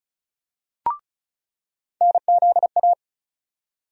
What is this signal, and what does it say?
Morse code, heard as a keyed tone, about 36 wpm, 700 Hz. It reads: NZA
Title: USA Call Sign Postfixes - 3 Letters 35wpm